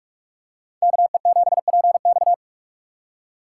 Morse code, heard as a keyed tone, about 42 wpm, 700 Hz. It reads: KE7PX